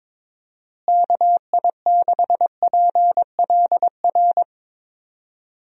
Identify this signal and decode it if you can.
Morse code, heard as a keyed tone, about 22 wpm, 700 Hz. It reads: KI6PLR
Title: USA Call Signs - 6 Characters - Repeated 2x - 22wpm